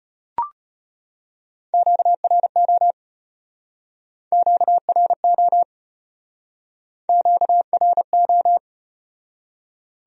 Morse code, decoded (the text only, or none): QRO QRO QRO